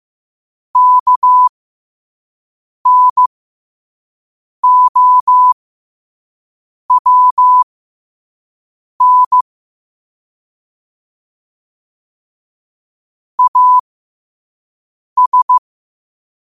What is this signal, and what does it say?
Morse code, heard as a keyed tone, about 15 wpm, spaced as for 5 wpm, 1000 Hz. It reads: KNOWN AS